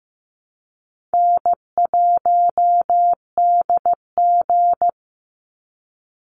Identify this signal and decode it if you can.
Morse code, heard as a keyed tone, about 15 wpm, 700 Hz. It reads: N1DG
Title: Easy Call Signs Repeated 2x - 15wpm